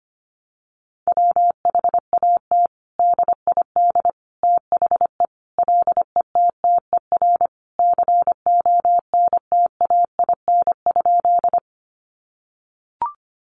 Morse code, heard as a keyed tone, about 25 wpm, 700 Hz. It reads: WHAT DID THE LETTER CONTAIN?
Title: Sentences from Top 500 Words 25wpm